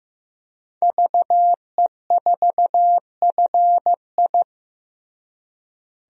Morse code, heard as a keyed tone, about 15 wpm, 700 Hz. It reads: VE4FI
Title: DXCC-100 Call Signs - 5 Characters - Repeated 3x - 15wpm